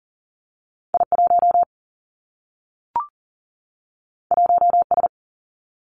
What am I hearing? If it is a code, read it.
Morse code, heard as a keyed tone, about 43 wpm, 700 Hz. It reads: I 1 1 S
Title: Two Challenging Letter-Number Combinations 40wpm